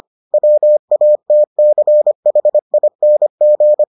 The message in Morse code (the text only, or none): WATCHING